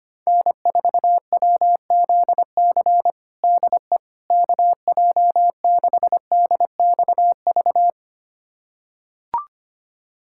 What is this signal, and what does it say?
Morse code, heard as a keyed tone, about 25 wpm, 700 Hz. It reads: N4WZC DE KJ6DXV
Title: USA Callsign DE Callsign 25wpm